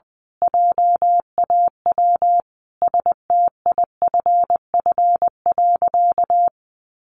Morse code, heard as a keyed tone, about 20 wpm, 700 Hz. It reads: JAW STIFF.
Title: A Sound Of Thunder 20wpm